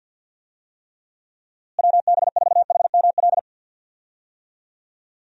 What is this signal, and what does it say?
Morse code, heard as a keyed tone, about 52 wpm, 700 Hz. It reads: WB4HML
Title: USA Call Signs - 6 Characters - Repeated 2x - 50wpm